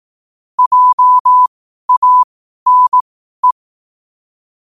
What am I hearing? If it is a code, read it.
Morse code, heard as a keyed tone, about 18 wpm, spaced as for 12 wpm, 1000 Hz. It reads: JANE
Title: 12 WPM morse code quotes for Thu, 29 Feb 2024